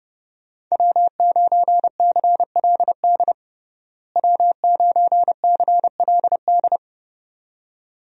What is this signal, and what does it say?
Morse code, heard as a keyed tone, about 30 wpm, 700 Hz. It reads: W9CLD W9CLD
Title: USA Call Signs - 5 Characters - Repeated 2x - 30wpm